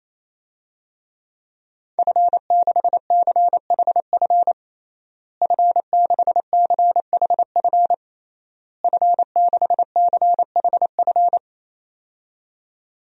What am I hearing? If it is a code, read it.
Morse code, heard as a keyed tone, about 28 wpm, 700 Hz. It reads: F6CHF F6CHF F6CHF
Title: DXCC-100 Call Signs - 5 Characters - Repeated 3x - 28wpm